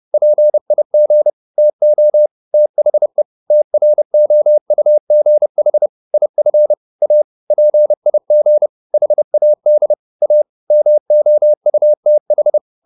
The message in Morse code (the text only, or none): PIG TO THE TROUGH IF A PIG HAD A MOUTH